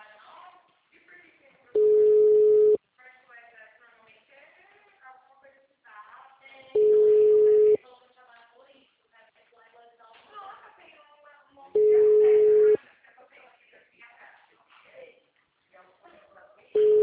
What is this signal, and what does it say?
Ringback tone (Europe, 425 Hz), with speech.
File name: out- -2015- - - .990